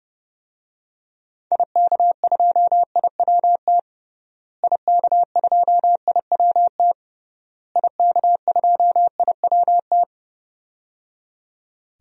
Morse code, heard as a keyed tone, about 30 wpm, 700 Hz. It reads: IK2IWT IK2IWT IK2IWT
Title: DXCC-100 Call Signs - 6 Characters - Repeated 3x - 30wpm